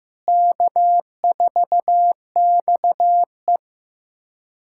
Morse code, heard as a keyed tone, about 16 wpm, 700 Hz. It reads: K4XE
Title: USA Callsign DE Callsign 15wpm